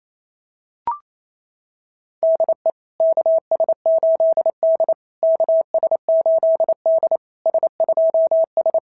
Morse code, heard as a keyed tone, about 27 wpm, 650 Hz. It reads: DE KS8D KS8D S2S
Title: SOTA QSO Practice 28wpm